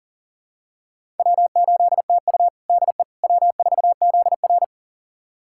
Morse code, heard as a keyed tone, about 40 wpm, 700 Hz. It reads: W8TU DE W4ZR
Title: USA Callsign DE Callsign - 4 Characters - 40wpm